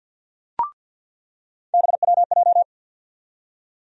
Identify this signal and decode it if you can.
Morse code, heard as a keyed tone, about 47 wpm, 700 Hz. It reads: BWJ
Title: USA Call Sign Postfixes - 3 Letters 50wpm